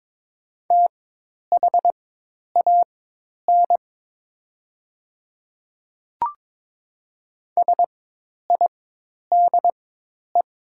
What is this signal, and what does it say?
Morse code, heard as a keyed tone, about 22 wpm, spaced as for 10 wpm, 700 Hz. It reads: THAN SIDE
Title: 4x Character Spacing - Top 100 Words - 22wpm